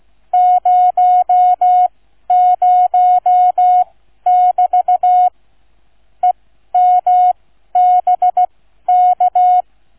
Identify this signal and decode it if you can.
Morse code, heard as a keyed tone, about 16 wpm, spaced as for 12 wpm, 730 Hz. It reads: 00<BT> EMBK